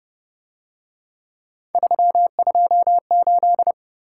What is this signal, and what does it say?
Morse code, heard as a keyed tone, about 29 wpm, 700 Hz. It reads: 328
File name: Speed-Racing - Three Letter-Number Combinations - 25wpm